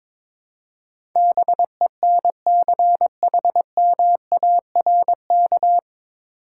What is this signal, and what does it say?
Morse code, heard as a keyed tone, about 22 wpm, 700 Hz. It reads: BENCHMARK
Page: Morse Code Ninja practice 22wpm